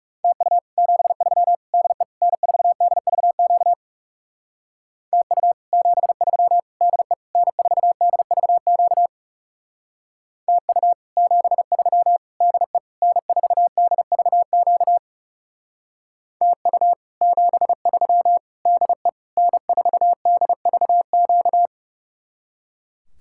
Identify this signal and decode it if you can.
Morse code, heard as a keyed tone, about 40 wpm, 700 Hz. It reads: TU 73 DE N4DVQ TU 73 DE N4DVQ TU 73 DE N4DVQ TU 73 DE N4DVQ